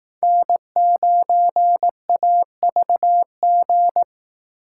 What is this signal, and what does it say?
Morse code, heard as a keyed tone, about 18 wpm, 700 Hz. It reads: N9AVG
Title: Speed Racing - USA 1x3 Call Signs 15wpm